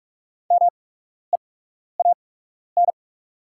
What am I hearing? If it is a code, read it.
Morse code, heard as a keyed tone, about 44 wpm, spaced as for 12 wpm, 700 Hz. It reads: MEAN
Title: 8x Character Spacing - Top 200 Words - 45wpm